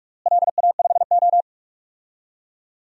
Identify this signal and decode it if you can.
Morse code, heard as a keyed tone, about 46 wpm, 700 Hz. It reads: LA5O
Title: DXCC-100 Call Signs - 4 Characters - Repeated 3x - 45wpm